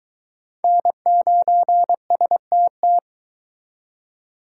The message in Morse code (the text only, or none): N9STT